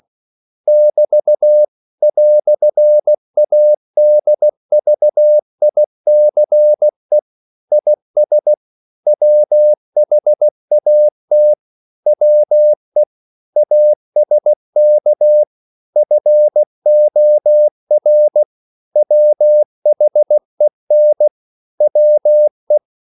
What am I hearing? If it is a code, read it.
Morse code, heard as a keyed tone, about 16 wpm, 600 Hz. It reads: <BT> "ADVICE IS WHAT WE ASK FOR WHEN WE
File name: CW / Practice / EnglishQuotations0216